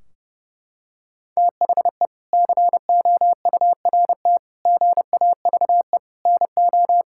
Morse code, heard as a keyed tone, about 30 wpm, 700 Hz. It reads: THE COURT GAVE NO
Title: Sentences from Top 1600 Words 30wpm